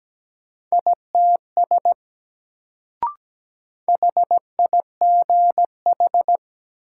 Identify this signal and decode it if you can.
Morse code, heard as a keyed tone, about 17 wpm, 700 Hz. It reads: ITS HIGH
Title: Sets of 2 Words - New Word Drill - 17wpm